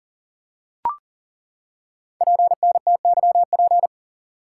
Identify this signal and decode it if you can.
Morse code, heard as a keyed tone, about 40 wpm, 690 Hz. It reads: PNTYP
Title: Code Groups - 5 Letter:Numbers - 40wpm